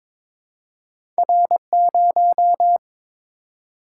Morse code, heard as a keyed tone, about 23 wpm, 700 Hz. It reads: R0
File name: practice / 22wpm / Two Letter-Number Combinations - 22wpm